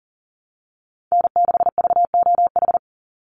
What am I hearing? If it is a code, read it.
Morse code, heard as a keyed tone, about 39 wpm, 700 Hz. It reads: N6VOH
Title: Call Signs - 40wpm